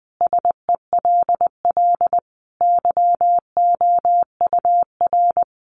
Morse code, heard as a keyed tone, about 19 wpm, 700 Hz. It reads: SELL YOUR
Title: Sentences from Top 1200 Words 20wpm